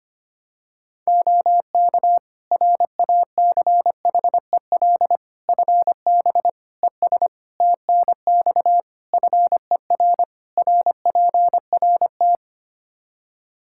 Morse code, heard as a keyed tone, about 25 wpm, 700 Hz. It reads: OK RACHEL FB ES TNX FER RPRT
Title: Practice QSO - Exchange 2 - Info - 25wpm